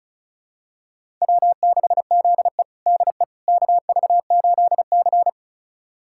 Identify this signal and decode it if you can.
Morse code, heard as a keyed tone, about 34 wpm, 700 Hz. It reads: W6ZE DE KV8C
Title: USA Callsign DE Callsign - 4 Characters - 35wpm